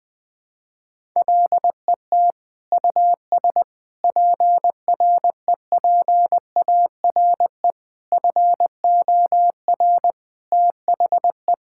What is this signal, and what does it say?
Morse code, heard as a keyed tone, about 21 wpm, 700 Hz. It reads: LET US PREPARE FOR THE
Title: Sentences from Top 1000 Words - Review Part 6 - 20wpm